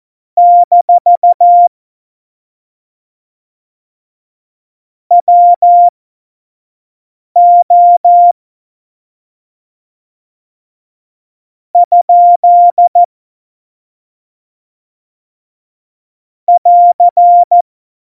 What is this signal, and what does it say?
Morse code, heard as a keyed tone, about 14 wpm, spaced as for 5 wpm, 700 Hz. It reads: - WO ? <AR>
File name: Morsekurs / Witze / Witz 26. Juni 2018, 25 F70